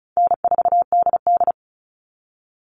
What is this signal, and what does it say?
Morse code, heard as a keyed tone, about 34 wpm, 700 Hz. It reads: N4DD
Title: All Cut Numbers - 4-Digit 35wpm